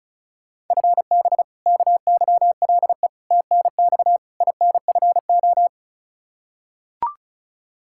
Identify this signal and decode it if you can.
Morse code, heard as a keyed tone, about 34 wpm, 700 Hz. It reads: FB KYLE TNX INFO
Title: Practice QSO - Exchange 3 - More Info - 35wpm